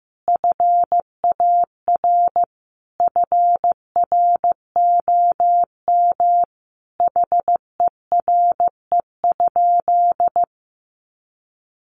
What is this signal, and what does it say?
Morse code, heard as a keyed tone, about 15 wpm, 700 Hz. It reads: FAR FROM HERE?
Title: Sentences from Top 1100 Words 15wpm